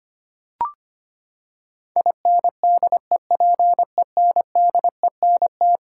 Morse code, heard as a keyed tone, about 25 wpm, 700 Hz. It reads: INDEPENDENT
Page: morse-code-ninja-practice-25wpm